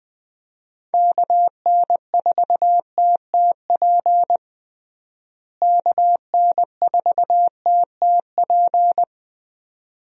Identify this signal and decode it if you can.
Morse code, heard as a keyed tone, about 21 wpm, 700 Hz. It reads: KN4TTP KN4TTP
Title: USA Call Signs - 6 Characters - Repeated 2x - 20wpm